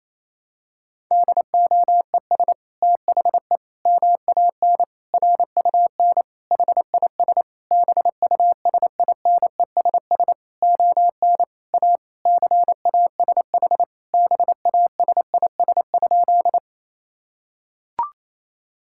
Morse code, heard as a keyed tone, about 28 wpm, 700 Hz. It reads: DOES THE MAN RUN HIS BUSINESS ON A CASH BASIS?
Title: Sentences from Top 1000 Words - Review Part 5 - 28wpm